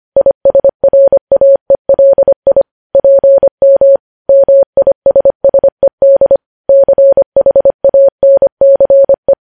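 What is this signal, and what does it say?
Morse code, heard as a keyed tone, about 25 wpm, 550 Hz. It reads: ISRAELI PM MISSED CHANCE
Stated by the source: Morse code news headlines sent at 25 wpm.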